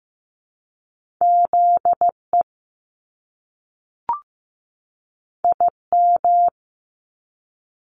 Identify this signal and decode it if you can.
Morse code, heard as a keyed tone, about 15 wpm, 700 Hz. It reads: ZE IM
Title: Top Two Letter Combinations in Words 15wpm